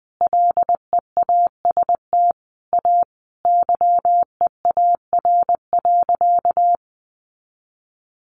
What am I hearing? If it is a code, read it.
Morse code, heard as a keyed tone, about 20 wpm, 700 Hz. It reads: LEAST A YEAR.
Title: Sentences from Top 700 Words 20wpm